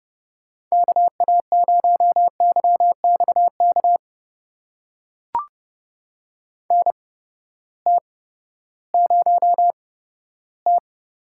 Morse code, heard as a keyed tone, about 31 wpm, spaced as for 15 wpm, 700 Hz. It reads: KA0YXK NT0T
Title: USA Call Signs - 8x Character Spacing - 30wpm